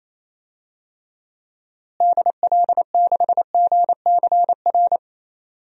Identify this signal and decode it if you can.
Morse code, heard as a keyed tone, about 27 wpm, 700 Hz. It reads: DL6GCR
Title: DXCC-100 Call Signs - 6 Characters - Repeated 3x - 28wpm